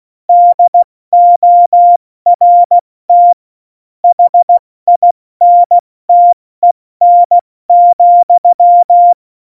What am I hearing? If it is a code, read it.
Morse code, heard as a keyed tone, about 16 wpm, 700 Hz. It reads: DORT HINTEN,
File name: Morsekurs / Witze / Witz 07 Juli 2020, 70 F80